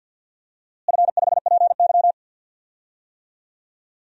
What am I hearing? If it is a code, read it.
Morse code, heard as a keyed tone, about 52 wpm, 700 Hz. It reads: F5PY
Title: DXCC-100 Call Signs - 4 Characters - Repeated 2x - 50wpm